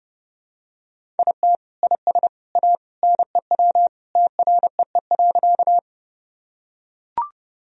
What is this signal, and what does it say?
Morse code, heard as a keyed tone, about 32 wpm, 700 Hz. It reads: IT IS A NEW TREE.
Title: Sentences from Top 1000 Words - Review Part 1 - 30wpm